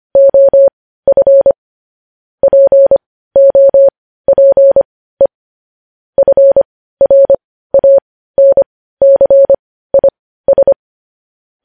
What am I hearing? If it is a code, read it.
Morse code, heard as a keyed tone, about 25 wpm, spaced as for 15 wpm, 550 Hz. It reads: OF POPE FRANCIS
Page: News Headlines in Morse Code at 15 WPM